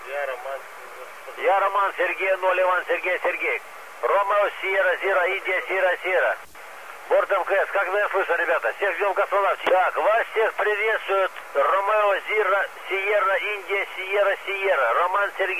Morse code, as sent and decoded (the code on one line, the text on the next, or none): none